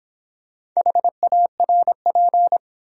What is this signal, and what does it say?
Morse code, heard as a keyed tone, about 26 wpm, 700 Hz. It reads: HARP